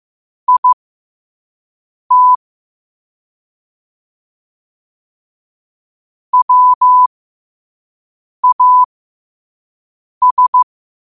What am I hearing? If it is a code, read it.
Morse code, heard as a keyed tone, about 15 wpm, spaced as for 5 wpm, 1000 Hz. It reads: IT WAS
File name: qotdfiles / 5wpm